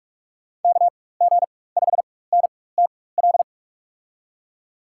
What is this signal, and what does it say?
Morse code, heard as a keyed tone, about 46 wpm, spaced as for 21 wpm, 700 Hz. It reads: KG5NTL
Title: USA Call Signs - 6 Characters - 4x Character Spacing - 45wpm